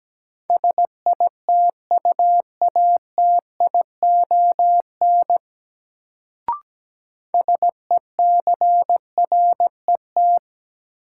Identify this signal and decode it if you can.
Morse code, heard as a keyed tone, about 17 wpm, 700 Hz. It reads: SITUATION SECRET
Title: Sentences from Top 1200 Words - New Word Drill - 17wpm